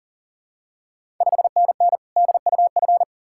Morse code, heard as a keyed tone, about 39 wpm, 690 Hz. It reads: 5NN DUF
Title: Ontario QSO Party Exchange 40wpm